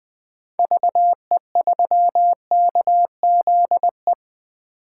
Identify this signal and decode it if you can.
Morse code, heard as a keyed tone, about 20 wpm, 700 Hz. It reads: VE3KZE
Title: 2023 POTA Activator Call Signs - 20wpm